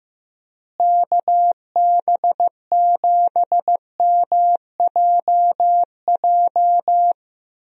Morse code, heard as a keyed tone, about 15 wpm, 700 Hz. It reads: KB7MJJ
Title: USA Call Signs - 6 Characters - Repeated 3x - 15wpm